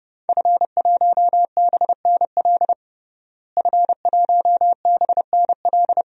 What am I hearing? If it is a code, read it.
Morse code, heard as a keyed tone, about 30 wpm, 700 Hz. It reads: F1BNL F1BNL
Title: DXCC-100 Call Signs - 5 Characters - Repeated 2x - 30wpm